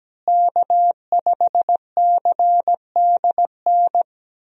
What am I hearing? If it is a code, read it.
Morse code, heard as a keyed tone, about 17 wpm, 700 Hz. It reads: K5CDN
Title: USA Call Signs - 5 Characters - Repeated 3x - 17wpm